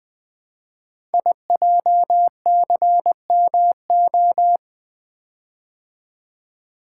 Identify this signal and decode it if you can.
Morse code, heard as a keyed tone, about 20 wpm, 700 Hz. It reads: IJCMO
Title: Code Groups - 5 Letters - 20wpm